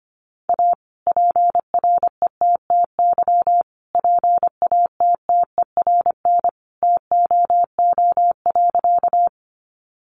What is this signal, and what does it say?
Morse code, heard as a keyed tone, about 25 wpm, 700 Hz. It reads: A PRETTY PATTERN TOO.
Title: Sentences from Top 1000 Words 25wpm